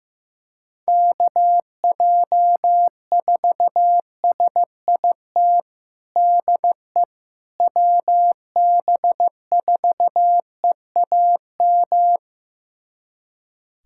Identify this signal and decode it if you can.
Morse code, heard as a keyed tone, about 15 wpm, 700 Hz. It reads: KJ4SIT DE WB4EAM
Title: USA Callsign DE Callsign - 6 Characters - 15wpm